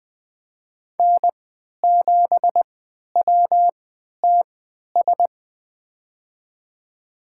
Morse code, heard as a keyed tone, about 20 wpm, spaced as for 11 wpm, 700 Hz. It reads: N7WTS
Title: USA Call Signs - 5 Characters - 3x Character Spacing - 20wpm